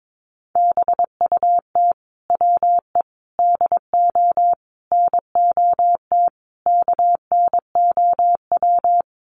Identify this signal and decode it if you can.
Morse code, heard as a keyed tone, about 22 wpm, 700 Hz. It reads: BUT WE DO NOT KNOW
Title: Sets of 5 Words Encore2 22wpm